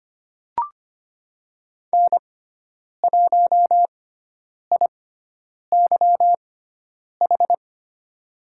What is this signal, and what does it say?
Morse code, heard as a keyed tone, about 25 wpm, spaced as for 9 wpm, 700 Hz. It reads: N1IYH
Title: USA Call Signs - 5 Characters - 6x Character Spacing - 25wpm